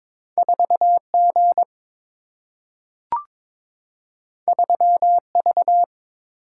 Morse code, heard as a keyed tone, about 22 wpm, 700 Hz. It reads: 4 G 3 V
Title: Single-Letter Single-Number 22wpm